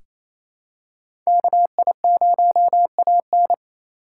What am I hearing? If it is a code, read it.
Morse code, heard as a keyed tone, about 28 wpm, 700 Hz. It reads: KI0AN
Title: USA Call Signs - 5 Characters - 2x Character Spacing - 28wpm